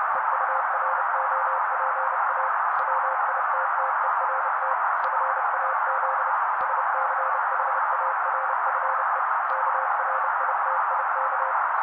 Morse code, heard as a keyed tone, about 30 wpm, 530 Hz. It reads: EUROPA WITTERT FAZ SCHARFE KRITIK
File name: daily / 29wpm noise